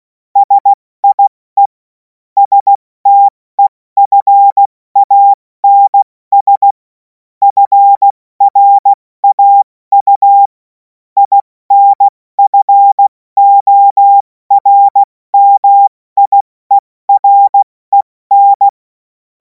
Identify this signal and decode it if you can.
Morse code, heard as a keyed tone, about 16 wpm, 800 Hz. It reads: SIE STEFANS FRAU INFORMIEREN
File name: Morsekurs / Witze / Witz 07. April 2026 70 F80